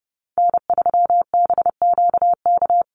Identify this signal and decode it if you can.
Morse code, heard as a keyed tone, about 30 wpm, 700 Hz. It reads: N3BQK
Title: USA Call Signs Repeated 2x 30wpm